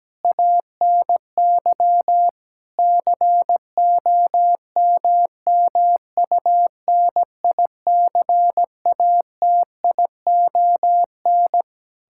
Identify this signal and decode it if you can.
Morse code, heard as a keyed tone, about 17 wpm, 700 Hz. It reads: ANY COMMUNICATION
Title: Sentences from Top 1000 Words - Review Part 6 - 17wpm